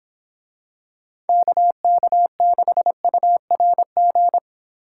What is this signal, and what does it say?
Morse code, heard as a keyed tone, about 26 wpm, 700 Hz. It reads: KK6URG